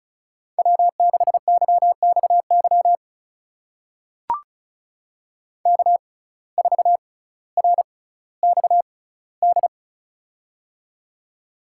Morse code, heard as a keyed tone, about 36 wpm, spaced as for 18 wpm, 700 Hz. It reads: W6YXY K4RXD
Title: USA Call Signs - 5 Characters - 6x Character Spacing - 35wpm